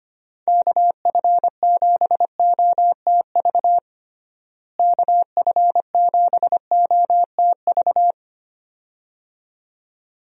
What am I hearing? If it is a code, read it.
Morse code, heard as a keyed tone, about 25 wpm, 700 Hz. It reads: KF7OTV KF7OTV
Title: USA Call Signs - 6 Characters - Repeated 2x - 25wpm